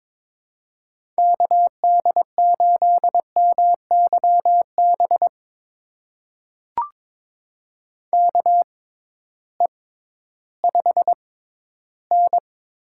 Morse code, heard as a keyed tone, about 22 wpm, spaced as for 13 wpm, 700 Hz. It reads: KD8MYB KE5N
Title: USA Call Signs - 6 Characters - 6x Character Spacing - 22wpm